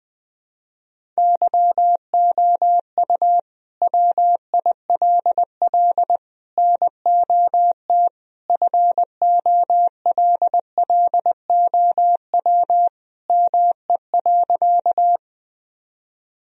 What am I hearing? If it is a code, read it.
Morse code, heard as a keyed tone, about 20 wpm, 700 Hz. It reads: YOU WILL NOT FOLLOW ME.
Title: Sentences from Top 1000 Words - Review Part 1 - 20wpm